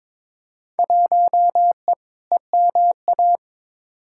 Morse code, heard as a keyed tone, about 21 wpm, 700 Hz. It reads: 1E EMA
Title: Field Day Exchanges Encore - 22wpm